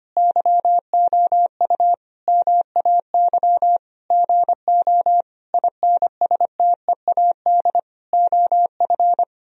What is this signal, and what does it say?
Morse code, heard as a keyed tone, about 25 wpm, 700 Hz. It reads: YOU MAY GO INSTEAD OF
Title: Sentences from Top 1000 Words - Review Part 4 - 25wpm